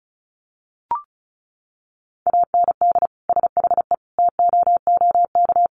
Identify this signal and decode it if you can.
Morse code, heard as a keyed tone, about 34 wpm, 700 Hz. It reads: AND SHE TOOK
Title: Sentences from Top 1200 Words 35wpm